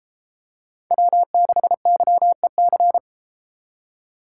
Morse code, heard as a keyed tone, about 31 wpm, 700 Hz. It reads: W6YEC